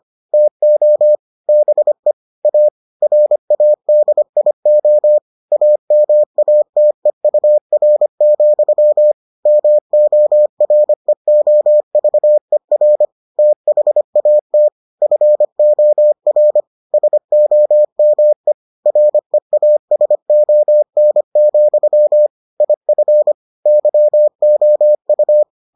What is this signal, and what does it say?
Morse code, heard as a keyed tone, about 25 wpm, 600 Hz. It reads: TO BE A RADIO AMATEUR, MOREOVER THAT FOR SOME REASON, IF YOU